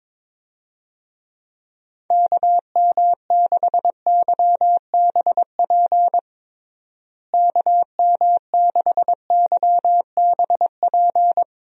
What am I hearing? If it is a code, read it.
Morse code, heard as a keyed tone, about 22 wpm, 700 Hz. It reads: KM6YBP KM6YBP
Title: USA Call Signs - 6 Characters - Repeated 3x - 22wpm